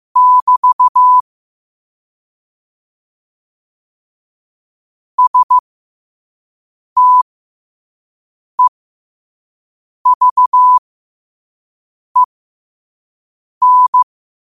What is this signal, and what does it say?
Morse code, heard as a keyed tone, about 15 wpm, spaced as for 5 wpm, 1000 Hz. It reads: <BT> STEVEN
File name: 5wpm